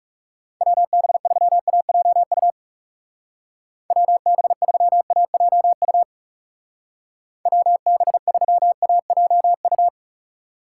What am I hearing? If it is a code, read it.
Morse code, heard as a keyed tone, about 37 wpm, 700 Hz. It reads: WB3AJU WB3AJU WB3AJU
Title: Speed Racing - USA 2x3 Call Signs 30wpm